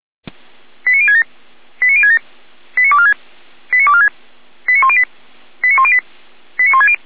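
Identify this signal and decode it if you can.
Five-tone selective call (ZVEI), heard as quick runs of five tones, tones of about 70 ms. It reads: 89067 89067 89246 89246 89108 89108 89160
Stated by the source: ZVEI five-tone selective call 89170 received on SDR, 455 channel.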